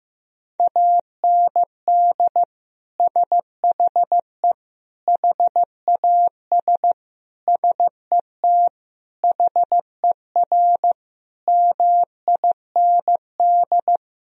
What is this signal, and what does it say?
Morse code, heard as a keyed tone, about 15 wpm, 700 Hz. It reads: AND SHE HAS SET HER MIND
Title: Sentences from Top 1000 Words - Review Part 4 - 15wpm